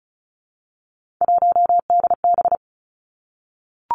Morse code, heard as a keyed tone, about 36 wpm, 700 Hz. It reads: 1DB
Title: Three Challenging Letter-Number Combinations 35wpm